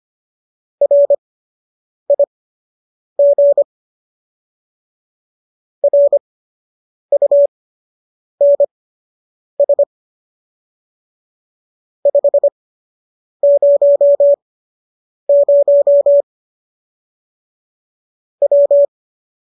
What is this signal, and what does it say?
Morse code, heard as a keyed tone, about 25 wpm, spaced as for 8 wpm, 570 Hz. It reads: RIG RUNS 500 W